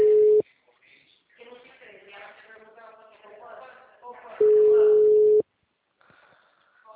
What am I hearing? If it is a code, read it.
Ringback tone (Europe, 425 Hz), with speech.
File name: out- -2013- - - .20125